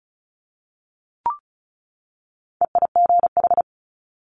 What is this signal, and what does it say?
Morse code, heard as a keyed tone, about 36 wpm, 700 Hz. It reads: EIGH
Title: Multi-letter Phonemes 35wpm